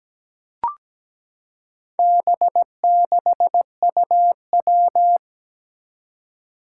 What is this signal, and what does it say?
Morse code, heard as a keyed tone, about 17 wpm, 700 Hz. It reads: B6UW
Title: Four Challenging Letter-Number Combinations 17wpm